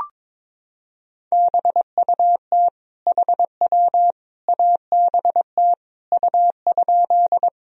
Morse code, heard as a keyed tone, about 22 wpm, 700 Hz. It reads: BUT HW ABT U?
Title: Sentences from Top 100 Words and Abbreviations 22wpm